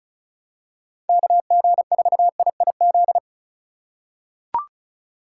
Morse code, heard as a keyed tone, about 36 wpm, 700 Hz. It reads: KG4IIZ
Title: USA Call Signs - 6 Characters - 3x Character Spacing - 35wpm